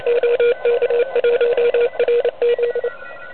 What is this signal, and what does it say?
Morse code, heard as a keyed tone, about 28 wpm, 480 Hz, with noise. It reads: OK1RZ
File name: truSDX / MWC29.8